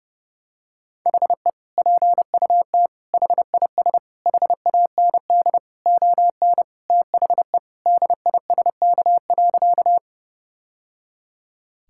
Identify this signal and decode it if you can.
Morse code, heard as a keyed tone, about 30 wpm, 700 Hz. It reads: HE PUT HIS HAND ON THE DISK.
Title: Sentences from Top 1000 Words - Review Part 6 - 30wpm